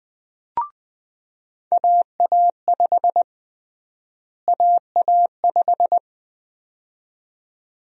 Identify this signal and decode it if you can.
Morse code, heard as a keyed tone, about 21 wpm, 700 Hz. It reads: AA5 AA5
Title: USA Call Sign Prefixes - Repeated 2x - 20wpm